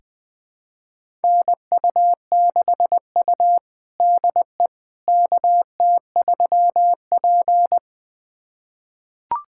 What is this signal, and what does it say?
Morse code, heard as a keyed tone, about 20 wpm, 700 Hz. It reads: NU6U DE KT3P
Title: USA Callsign DE Callsign - 4 Characters - 20wpm